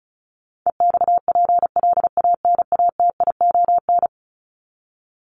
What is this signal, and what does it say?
Morse code, heard as a keyed tone, about 34 wpm, 700 Hz. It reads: EXPLANATION
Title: Top 1751-2000 Words 35wpm